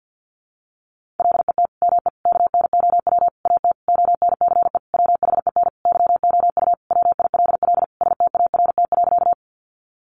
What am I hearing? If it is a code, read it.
Morse code, heard as a keyed tone, about 50 wpm, 700 Hz. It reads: LET ME KNOW AT ONCE WHEN YOU WILL START.
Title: Sentences from Top 300 Words 50wpm